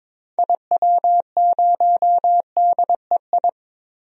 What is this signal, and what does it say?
Morse code, heard as a keyed tone, about 22 wpm, 700 Hz. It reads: IW0DEI
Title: DXCC-100 Call Signs - 6 Characters - 22wpm